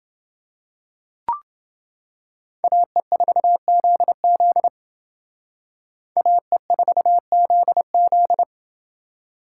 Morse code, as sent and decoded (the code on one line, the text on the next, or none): .- . ....- --.. --.. / .- . ....- --.. --..
AE4ZZ AE4ZZ